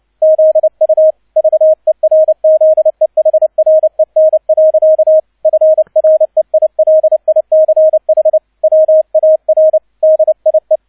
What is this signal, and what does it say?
Morse code, heard as a keyed tone, about 29 wpm, 630 Hz. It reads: ZU VERZEHREN. FREILICH WAR DIE
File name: 002 Zweiter Streich 30wpm